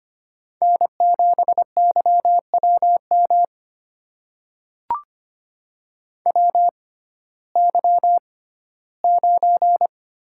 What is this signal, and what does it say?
Morse code, heard as a keyed tone, about 25 wpm, spaced as for 15 wpm, 700 Hz. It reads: N7YWM WY9
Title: USA Call Signs - 6x Character Spacing - 25wpm